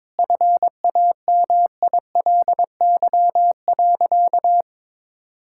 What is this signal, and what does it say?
Morse code, heard as a keyed tone, about 22 wpm, 700 Hz. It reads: FAMILY.
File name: Sentences with 7-Word Phrases - 22wpm